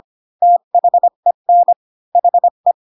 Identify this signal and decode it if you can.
Morse code, heard as a keyed tone, about 25 wpm, 700 Hz. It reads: THEN HE
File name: Pinocchio / Pinocchio ch7 20 25 ik1hge